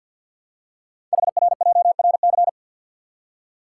Morse code, heard as a keyed tone, about 50 wpm, 700 Hz. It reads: HRJRC
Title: Code Groups - 5 Letters - 50wpm